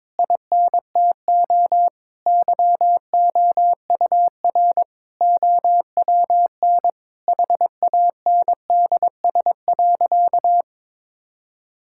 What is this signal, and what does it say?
Morse code, heard as a keyed tone, about 22 wpm, 700 Hz. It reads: INTO YOUR OWN HANDS.
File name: Sentences with 7-Word Phrases - 22wpm